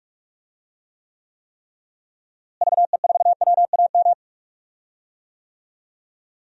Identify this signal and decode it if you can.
Morse code, heard as a keyed tone, about 42 wpm, 700 Hz. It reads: VE4WAM
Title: DXCC-100 Call Signs - 6 Characters - Repeated 3x - 45wpm